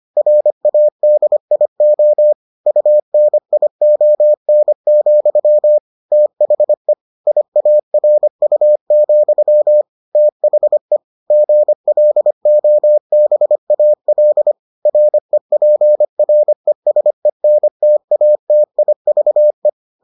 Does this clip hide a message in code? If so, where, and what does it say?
Yes: Morse code from 0.2 to 19.7 s, heard as a keyed tone, about 25 wpm, 600 Hz. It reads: RADIO UNION, THE IARU, THE GLOBAL REPRESENTATIVE